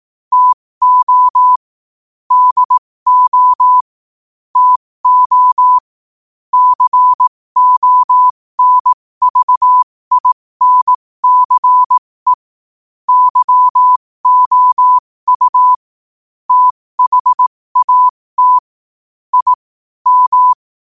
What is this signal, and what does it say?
Morse code, heard as a keyed tone, about 18 wpm, spaced as for 15 wpm, 1000 Hz. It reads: TO DO TO CONVINCE YOU THAT I M
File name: qotdfiles / 15wpm